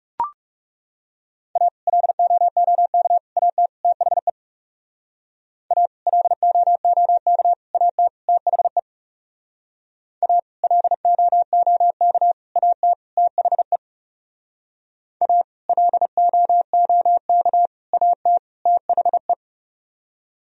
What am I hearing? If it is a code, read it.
Morse code, heard as a keyed tone, about 35 wpm, 700 Hz. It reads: A LOOK AT THE A LOOK AT THE A LOOK AT THE A LOOK AT THE
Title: Speed-Racing - Sets of 4 Words 30wpm